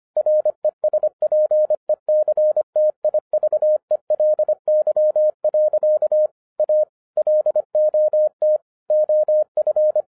Morse code, heard as a keyed tone, about 25 wpm, 610 Hz. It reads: RESPECTIVELY. A LOT OF
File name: 2011-07-mp3 / n24 QRP OTAKU 25wpm